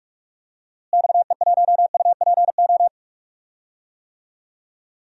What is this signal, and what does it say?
Morse code, heard as a keyed tone, about 44 wpm, 700 Hz. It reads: XE1UPO
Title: DXCC-100 Call Signs - 6 Characters - Repeated 3x - 45wpm